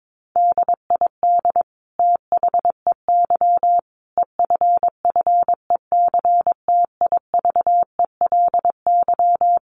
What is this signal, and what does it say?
Morse code, heard as a keyed tone, about 22 wpm, 700 Hz. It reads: DID THEY EFFECTIVELY
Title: Sentences from Top 1400 Words 22wpm